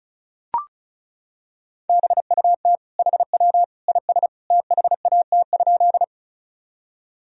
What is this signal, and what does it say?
Morse code, heard as a keyed tone, about 34 wpm, 700 Hz. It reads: BUT HW IS THAT?
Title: Sentences from Top 100 Words and Abbreviations 35wpm